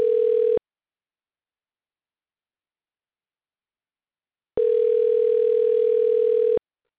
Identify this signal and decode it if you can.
Ringback tone (North America), 440 + 480 Hz.